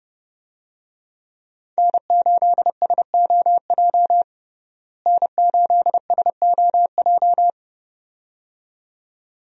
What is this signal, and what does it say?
Morse code, heard as a keyed tone, about 30 wpm, 700 Hz. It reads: N8SOJ N8SOJ
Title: USA Call Signs - 5 Characters - Repeated 2x - 30wpm